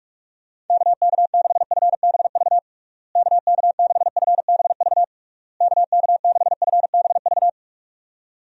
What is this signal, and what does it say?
Morse code, heard as a keyed tone, about 42 wpm, 700 Hz. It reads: KK6FBV KK6FBV KK6FBV
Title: USA Call Signs - 6 Characters - Repeated 3x - 45wpm